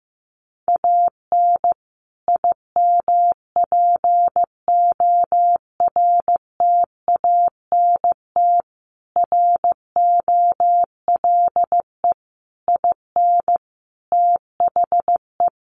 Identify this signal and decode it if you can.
Morse code, heard as a keyed tone, about 14 wpm, 700 Hz. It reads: AN IMPORTANT ROLE IN THE
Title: Sets of 5 Words Encore2 15wpm